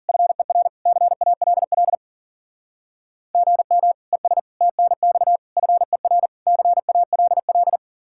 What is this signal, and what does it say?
Morse code, heard as a keyed tone, about 51 wpm, 700 Hz. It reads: FER CALL GM ES TNX FER CALL